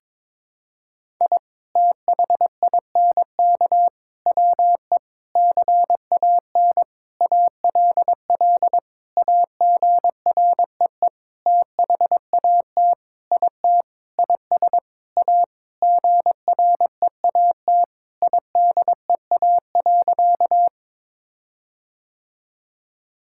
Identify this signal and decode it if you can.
Morse code, heard as a keyed tone, about 22 wpm, 700 Hz. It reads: I THINK WE CAN ALL AGREE THAT IT IS A GREAT IDEA.
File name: Sentences with 7-Word Phrases - 22wpm